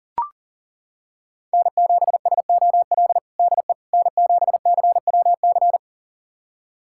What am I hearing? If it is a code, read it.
Morse code, heard as a keyed tone, about 37 wpm, 700 Hz. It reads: N7SOL DE N7CWC
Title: USA Callsign DE Callsign - 5 Characters - 40wpm